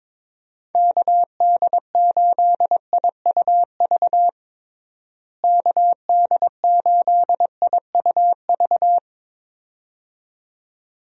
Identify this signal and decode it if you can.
Morse code, heard as a keyed tone, about 22 wpm, 700 Hz. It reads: KD8IUV KD8IUV
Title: USA Call Signs - 6 Characters - Repeated 2x - 22wpm